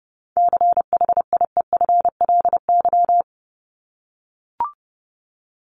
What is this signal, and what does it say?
Morse code, heard as a keyed tone, about 30 wpm, 700 Hz. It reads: CHIEFLY
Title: LY Suffix - Top 100 Words with Base Words 30wpm